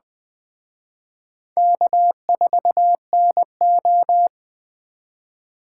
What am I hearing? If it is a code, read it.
Morse code, heard as a keyed tone, about 20 wpm, 700 Hz. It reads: K4NO
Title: Speed Racing - USA 1x2 Call Signs 17wpm